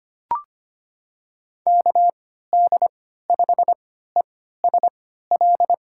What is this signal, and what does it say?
Morse code, heard as a keyed tone, about 25 wpm, spaced as for 14 wpm, 700 Hz. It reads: KD5ESL
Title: USA Call Signs - 6 Characters - 3x Character Spacing - 25wpm